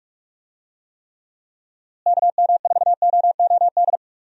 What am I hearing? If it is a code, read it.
Morse code, heard as a keyed tone, about 46 wpm, 700 Hz. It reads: KM4OOD
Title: USA Call Signs - 6 Characters - Repeated 3x - 45wpm